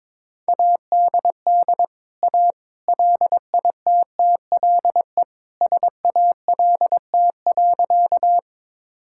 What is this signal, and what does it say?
Morse code, heard as a keyed tone, about 22 wpm, 700 Hz. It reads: ADD A LITTLE SALT.
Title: Sentences from Top 1000 Words - Review Part 5 - 22wpm